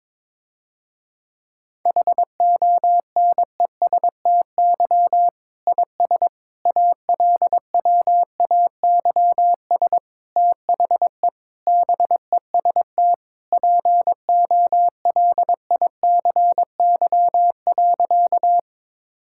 Morse code, heard as a keyed tone, about 22 wpm, 700 Hz. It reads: HONESTY IS ALWAYS THE BEST POLICY.
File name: Difficult Sentences with Hints - 22wpm